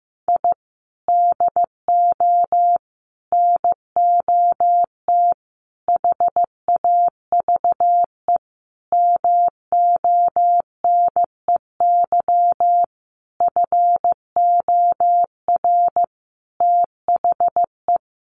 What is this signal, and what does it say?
Morse code, heard as a keyed tone, about 15 wpm, 700 Hz. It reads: I DO NOT HAVE MONEY FOR THE
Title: Sentences from Top 1400 Words 15wpm